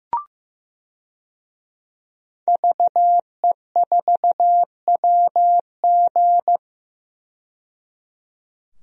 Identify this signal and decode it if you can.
Morse code, heard as a keyed tone, about 15 wpm, 700 Hz. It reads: VE4WG
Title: DXCC-100 Call Signs - 5 Characters - 15wpm